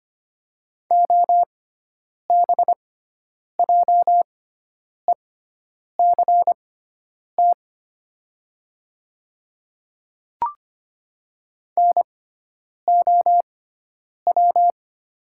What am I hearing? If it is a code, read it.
Morse code, heard as a keyed tone, about 25 wpm, spaced as for 9 wpm, 700 Hz. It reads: OBJECT NOW
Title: 6x Character Spacing - Top 500 Words - 25wpm